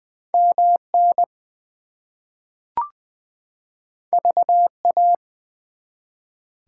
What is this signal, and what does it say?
Morse code, heard as a keyed tone, about 20 wpm, 700 Hz. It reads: MN VA
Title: US States and Canadian Provinces 20wpm